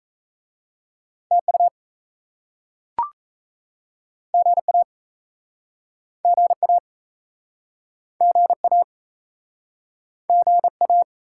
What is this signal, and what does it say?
Morse code, heard as a keyed tone, about 38 wpm, 700 Hz. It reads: TU GA GA GA GA